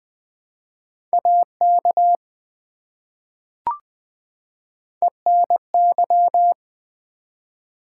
Morse code, heard as a keyed tone, about 20 wpm, 700 Hz. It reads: AK ENY
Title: ARRL Sections v2 - 20wpm